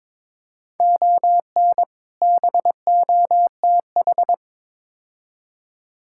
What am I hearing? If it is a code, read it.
Morse code, heard as a keyed tone, about 22 wpm, 700 Hz. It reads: ON BOTH
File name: practice / 22wpm / Build A Phrase - 5-Word Phrases - 22wpm